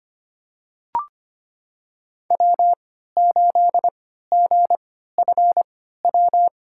Morse code, heard as a keyed tone, about 25 wpm, spaced as for 14 wpm, 700 Hz. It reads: W8GFW
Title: USA Call Signs - 5 Characters - 3x Character Spacing - 25wpm